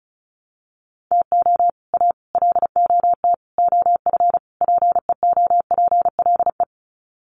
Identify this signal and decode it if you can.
Morse code, heard as a keyed tone, about 34 wpm, 700 Hz. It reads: TO A LOT OF PEOPLE
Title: Sets of 5 Words Encore 35wpm